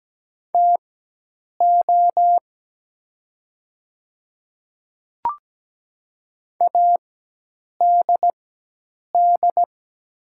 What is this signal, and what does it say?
Morse code, heard as a keyed tone, about 17 wpm, spaced as for 8 wpm, 700 Hz. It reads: TO ADD
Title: 4x Character Spacing - Top 200 Words - 17wpm